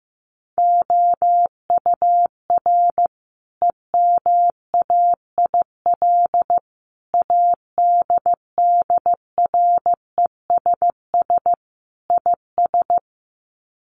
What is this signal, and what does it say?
Morse code, heard as a keyed tone, about 15 wpm, 700 Hz. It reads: OUR EMAIL ADDRESS IS
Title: Sets of 5 Words Encore2 15wpm